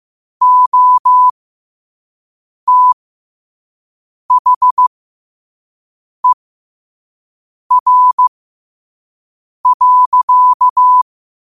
Morse code, heard as a keyed tone, about 15 wpm, spaced as for 5 wpm, 1000 Hz. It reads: OTHER.
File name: qotdfiles / 5wpm